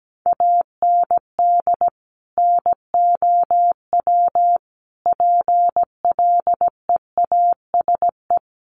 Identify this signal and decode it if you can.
Morse code, heard as a keyed tone, about 17 wpm, 700 Hz. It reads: AND NOW PLEASE
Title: Sentences from Top 1500 Words 17wpm